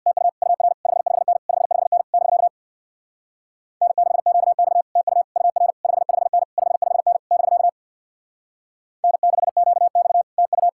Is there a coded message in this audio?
Yes: Morse code from 0.1 to 10.7 s, heard as a keyed tone, about 66 wpm, 700 Hz. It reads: TU UR 55N 55N <BK> N6QX TU UR 55N 55N <BK> N6QX TU